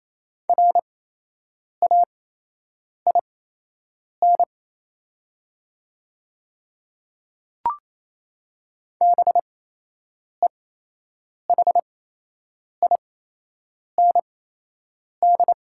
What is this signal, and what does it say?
Morse code, heard as a keyed tone, about 28 wpm, spaced as for 8 wpm, 700 Hz. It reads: RAIN BEHIND
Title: 8x Character Spacing - Top 500 Words - 28wpm